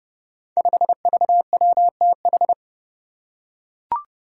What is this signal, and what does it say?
Morse code, heard as a keyed tone, about 30 wpm, 700 Hz. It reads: 5VWTH
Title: Five Challenging Letter-Number Combinations 30wpm